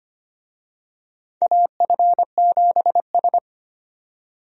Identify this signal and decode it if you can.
Morse code, heard as a keyed tone, about 25 wpm, 700 Hz. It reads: AF7S
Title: USA Call Signs - 4 Characters - 25wpm